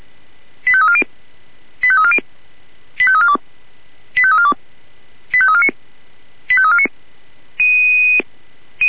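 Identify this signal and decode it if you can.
Five-tone selective call (ZVEI), heard as quick runs of five tones, tones of about 70 ms. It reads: 85340 85340 85342 85342 85349 85349
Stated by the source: ZVEI five-tone selective call received on an SDR.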